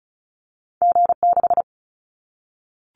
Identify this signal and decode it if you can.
Morse code, heard as a keyed tone, about 34 wpm, 700 Hz. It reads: G6
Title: Callsign Prefixes for Top 25 DXCC - Repeated 3x - 35wpm